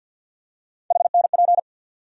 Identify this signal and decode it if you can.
Morse code, heard as a keyed tone, about 52 wpm, 700 Hz. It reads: HNP
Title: USA Call Sign Postfixes - 3 Letters 50wpm